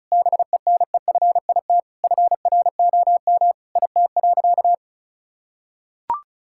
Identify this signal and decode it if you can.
Morse code, heard as a keyed tone, about 36 wpm, 700 Hz. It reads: BENEFIT FROM IT.
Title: Sentences from Top 1000 Words - Review Part 6 - 35wpm